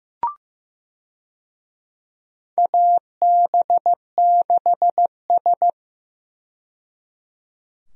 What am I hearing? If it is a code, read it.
Morse code, heard as a keyed tone, about 15 wpm, 700 Hz. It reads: AB6S
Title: USA Call Signs - 4 Characters - 15wpm